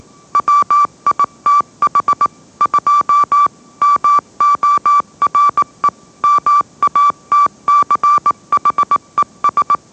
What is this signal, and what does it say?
Morse code, heard as a keyed tone, about 18 wpm, 1160 Hz. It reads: WITH 2 MORE MATCHES
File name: mrx / mrx081 20wpm